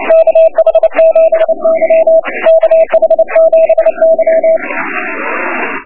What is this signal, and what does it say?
Morse code, heard as a keyed tone, about 29 wpm, 650 Hz. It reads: KH7Y KH7Y
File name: Morse on 28Mhz B